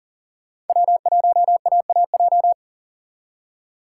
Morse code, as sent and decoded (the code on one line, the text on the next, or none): .-- .---- .- .- .---
W1AAJ